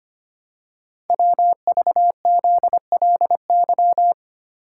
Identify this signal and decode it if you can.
Morse code, heard as a keyed tone, about 25 wpm, 700 Hz. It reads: WVZLY
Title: Code Groups - 5 Letters - 25wpm